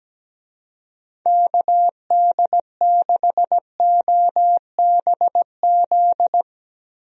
Morse code, heard as a keyed tone, about 17 wpm, 700 Hz. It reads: KD6OBZ
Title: USA Call Signs - 6 Characters - 17wpm